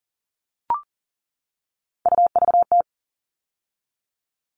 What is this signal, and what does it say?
Morse code, heard as a keyed tone, about 40 wpm, 700 Hz. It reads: UVT
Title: All Cut Numbers - 3-Digit 40wpm